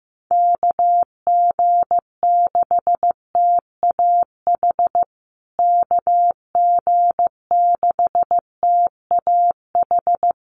Morse code, heard as a keyed tone, about 15 wpm, 700 Hz. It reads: KG6TAH KG6TAH
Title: USA Call Signs Repeated 3x 15wpm